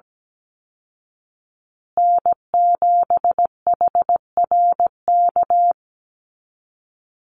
Morse code, heard as a keyed tone, about 17 wpm, 700 Hz. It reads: N7HRK
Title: Call Signs Repeated 3x - 17wpm